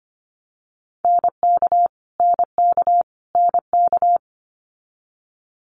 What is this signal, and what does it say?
Morse code, heard as a keyed tone, about 25 wpm, 700 Hz. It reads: NK NK NK
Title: Multi-letter Phonemes Repeated 3x 25wpm